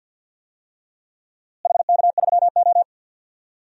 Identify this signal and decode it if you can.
Morse code, heard as a keyed tone, about 48 wpm, 700 Hz. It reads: HK3O
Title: DXCC-100 Call Signs - 4 Characters - Repeated 3x - 50wpm